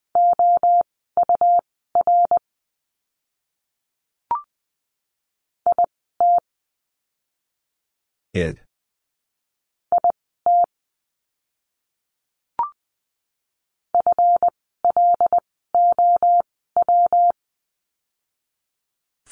Morse code, heard as a keyed tone, about 20 wpm, spaced as for 14 wpm, 700 Hz. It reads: OUR IT IT FLOW
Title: 139 Lesson 23f - Letter f - Words 20wpm